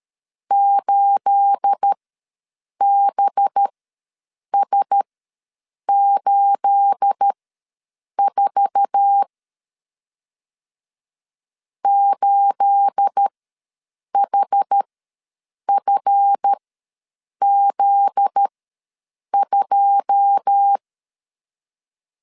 Morse code, heard as a keyed tone, about 13 wpm, spaced as for 7 wpm, 800 Hz. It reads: 8BS84 8HFZ2
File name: LEZIONI / 9 28 ws 12wpm 3354